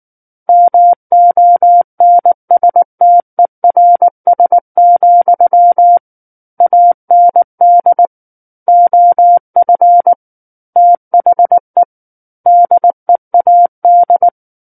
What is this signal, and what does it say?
Morse code, heard as a keyed tone, about 19 wpm, 700 Hz. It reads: MONSTERS, AND OF THE DEAD
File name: Book One / 10-Chapter 10 - 19 WPM